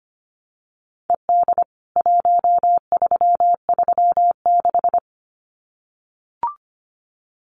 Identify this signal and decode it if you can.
Morse code, heard as a keyed tone, about 25 wpm, 700 Hz. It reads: ED 1336
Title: CWT Exchange 25wpm